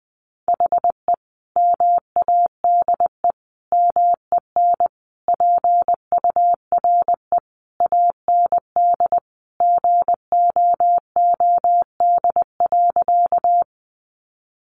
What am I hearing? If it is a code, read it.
Morse code, heard as a keyed tone, about 20 wpm, 700 Hz. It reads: HE MADE MEN PURE AND GOOD.
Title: Sentences from Top 1500 Words 20wpm